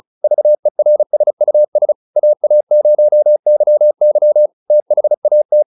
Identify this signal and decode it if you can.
Morse code, heard as a keyed tone, about 34 wpm, 600 Hz. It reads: VERSUS AA0YY THAT